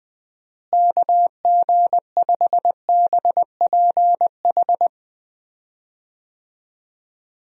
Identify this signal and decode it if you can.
Morse code, heard as a keyed tone, about 21 wpm, 700 Hz. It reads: KG5BPH
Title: USA Call Signs - 6 Characters - Repeated 3x - 20wpm